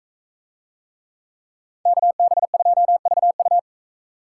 Morse code, heard as a keyed tone, about 41 wpm, 700 Hz. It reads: KB2VU